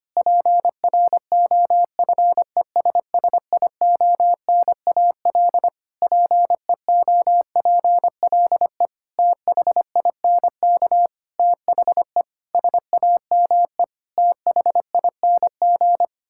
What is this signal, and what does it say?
Morse code, heard as a keyed tone, about 25 wpm, 700 Hz. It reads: PROFESSIONAL PEOPLE THINK THE SAME THING
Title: Sentences from Top 1000 Words - Review Part 5 - 25wpm